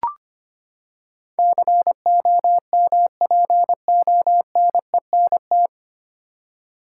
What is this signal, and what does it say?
Morse code, heard as a keyed tone, about 25 wpm, 700 Hz. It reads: COMPONENT
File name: Nine Letter Words - 25wpm